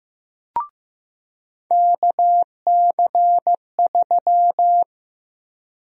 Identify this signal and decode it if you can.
Morse code, heard as a keyed tone, about 15 wpm, 700 Hz. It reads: KC3
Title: USA Call Sign Prefixes - Repeated 2x - 15wpm